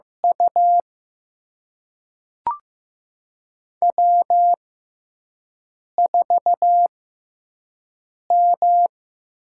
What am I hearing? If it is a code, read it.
Morse code, heard as a keyed tone, about 15 wpm, spaced as for 5 wpm, 700 Hz. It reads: U W4M
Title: USA Call Signs - 4 Characters - 6x Character Spacing - 15wpm